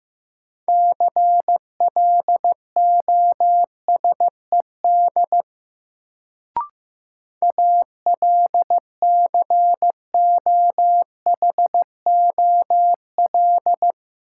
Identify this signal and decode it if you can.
Morse code, heard as a keyed tone, about 15 wpm, 700 Hz. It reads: CLOSED ALCOHOL
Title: Sentences from Top 1400 Words - New Word Drill - 15wpm